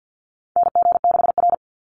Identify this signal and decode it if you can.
Morse code, heard as a keyed tone, about 49 wpm, 700 Hz. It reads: NG6R
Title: NA SOTA Chaser Callsigns 50wpm